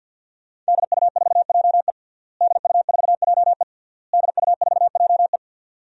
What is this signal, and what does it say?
Morse code, heard as a keyed tone, about 47 wpm, 700 Hz. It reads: DU4JE DU4JE DU4JE
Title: DXCC-100 Call Signs - 5 Characters - Repeated 3x - 50wpm